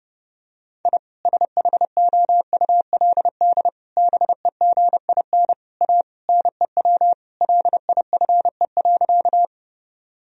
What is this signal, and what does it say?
Morse code, heard as a keyed tone, about 30 wpm, 700 Hz. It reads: I SHOULD BEGIN A NEW LIFE.
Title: Sentences from Top 1000 Words - Review Part 1 - 30wpm